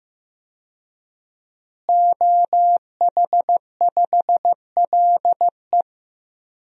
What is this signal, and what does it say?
Morse code, heard as a keyed tone, about 16 wpm, 700 Hz. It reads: OH5LE
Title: DXCC-100 Call Signs - 5 Characters - Repeated 2x - 15wpm